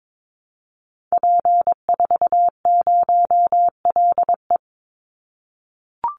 Morse code, heard as a keyed tone, about 22 wpm, 700 Hz. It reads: P40LE
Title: Call Signs - 22wpm